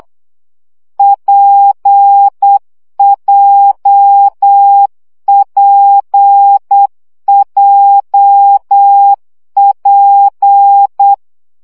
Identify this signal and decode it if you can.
Morse code, heard as a keyed tone, about 8 wpm, 800 Hz. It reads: PJPJP
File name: CW gyakorlas w p j 7 baud